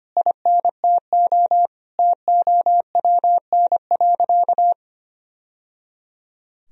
Morse code, heard as a keyed tone, about 25 wpm, 700 Hz. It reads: INTO TOWN.
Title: Sentences from Top 1000 Words - Review Part 1 - 25wpm